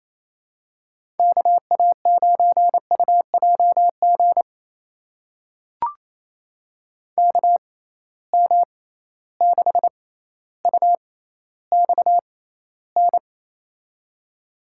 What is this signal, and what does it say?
Morse code, heard as a keyed tone, about 28 wpm, spaced as for 14 wpm, 700 Hz. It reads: KA9UJG KM6UXN
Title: USA Call Signs - 6 Characters - 6x Character Spacing - 28wpm